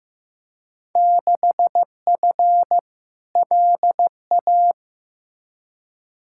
Morse code, heard as a keyed tone, about 15 wpm, 700 Hz. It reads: 6F LA
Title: Field Day Exchanges Encore - 15wpm